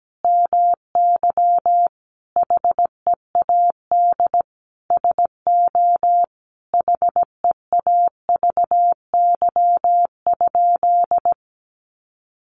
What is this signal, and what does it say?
Morse code, heard as a keyed tone, about 17 wpm, 700 Hz. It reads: MY HEAD SO HEAVY?
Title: Sentences from Top 1100 Words 17wpm